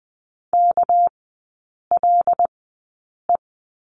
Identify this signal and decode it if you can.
Morse code, heard as a keyed tone, about 20 wpm, spaced as for 8 wpm, 700 Hz. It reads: KLE
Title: 218 Lesson 35e - Number 0 - Three Characters 20wpm